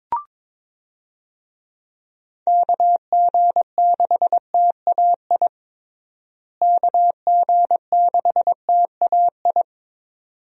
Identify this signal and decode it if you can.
Morse code, heard as a keyed tone, about 21 wpm, 700 Hz. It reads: KG6TAI KG6TAI
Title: USA Call Signs - 6 Characters - Repeated 2x - 22wpm